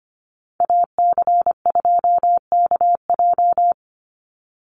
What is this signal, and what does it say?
Morse code, heard as a keyed tone, about 25 wpm, 700 Hz. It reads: AC2KJ
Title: Call Signs Repeated 2x - 25wpm